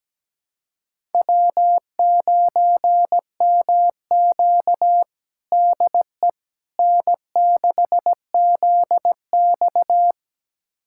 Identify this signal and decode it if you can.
Morse code, heard as a keyed tone, about 17 wpm, 700 Hz. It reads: W9MQ DE N6ZX
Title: USA Callsign DE Callsign - 4 Characters - 17wpm